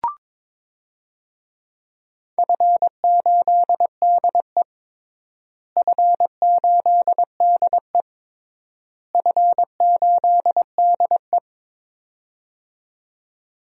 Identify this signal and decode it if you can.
Morse code, heard as a keyed tone, about 22 wpm, 700 Hz. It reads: F8DE F8DE F8DE
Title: DXCC-100 Call Signs - 4 Characters - Repeated 3x - 22wpm